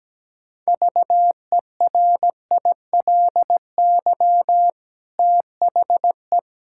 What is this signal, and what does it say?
Morse code, heard as a keyed tone, about 17 wpm, 700 Hz. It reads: VERILY THE
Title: Pangram Challenge v2 - 17wpm